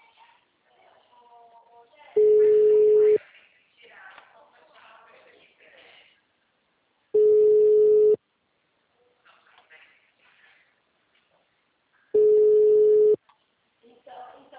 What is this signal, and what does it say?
Ringback tone (Europe, 425 Hz), with speech.
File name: out- -2008- - - .14